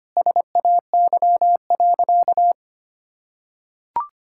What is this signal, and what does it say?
Morse code, heard as a keyed tone, about 25 wpm, 700 Hz. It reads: SAY.
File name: practice / 25wpm / Sentences with 2-Word Phrases - 25wpm